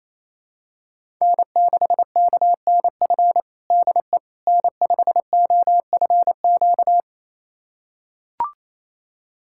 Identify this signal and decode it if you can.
Morse code, heard as a keyed tone, about 28 wpm, 700 Hz. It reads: N6KNF DE N5OFQ
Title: USA Callsign DE Callsign - 5 Characters - 28wpm